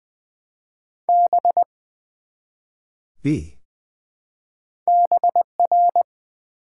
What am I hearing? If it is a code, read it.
Morse code, heard as a keyed tone, about 20 wpm, 700 Hz. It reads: B BR
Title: Build A Word - 1 Letter At A Time - 3 to 5-Letter Words - 20wpm